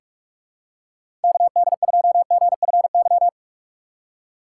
Morse code, heard as a keyed tone, about 46 wpm, 700 Hz. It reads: KD2GFY
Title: USA Call Signs - 6 Characters - 2x Character Spacing - 45wpm